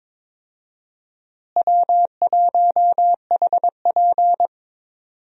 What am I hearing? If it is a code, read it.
Morse code, heard as a keyed tone, about 22 wpm, 700 Hz. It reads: W1HP
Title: USA Call Signs - 4 Characters - Repeated 3x - 22wpm